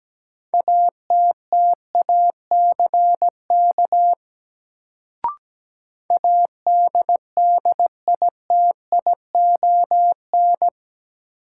Morse code, heard as a keyed tone, about 17 wpm, 700 Hz. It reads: ATTACK ADDITION
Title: Sentences from Top 1500 Words - New Word Drill Alt - 17wpm